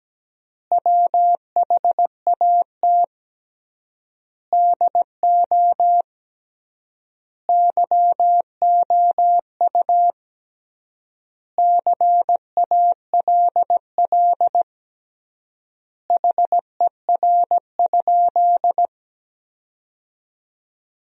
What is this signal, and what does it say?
Morse code, heard as a keyed tone, about 17 wpm, 700 Hz. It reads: WHAT DO YOU CALL HER?
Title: 3x Word-Spacing - Sentences from Top 100 Words - 17wpm